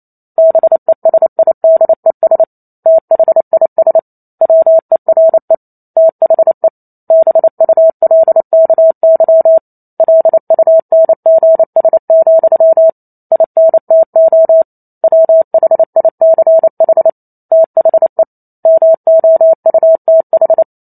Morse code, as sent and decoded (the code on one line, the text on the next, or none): -... ... .. -.. ... / - .... .. ... / .-- . .-. / - .... / -... ..- .-.. -.- -.-- / .-.. ..- -. --. ... --..-- / .. -. - --- / .-- .... .. -.-. .... / - .... / -- --- ..- - ....
BESIDES THIS WERE THE BULKY LUNGS, INTO WHICH THE MOUTH